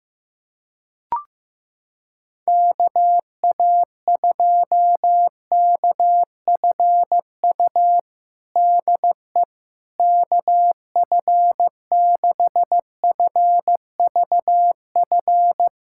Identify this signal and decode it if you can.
Morse code, heard as a keyed tone, about 15 wpm, 700 Hz. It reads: KA2KFU DE KF6FVF
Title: USA Callsign DE Callsign - 6 Characters - 15wpm